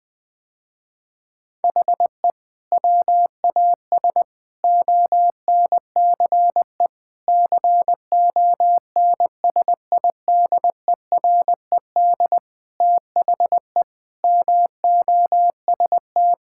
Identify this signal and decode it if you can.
Morse code, heard as a keyed tone, about 20 wpm, 700 Hz. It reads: HE WAS ONCE CONSIDERED THE MOST
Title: Sentences with 7-Word Phrases - Spoken Beginning - 20wpm